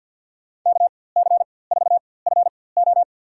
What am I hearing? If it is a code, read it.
Morse code, heard as a keyed tone, about 51 wpm, spaced as for 23 wpm, 700 Hz. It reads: KC4FO
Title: USA Call Signs - 5 Characters - 4x Character Spacing - 50wpm